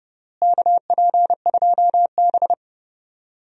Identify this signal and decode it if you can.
Morse code, heard as a keyed tone, about 31 wpm, 700 Hz. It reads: KP2B
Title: DXCC-100 Call Signs - 4 Characters - Repeated 3x - 30wpm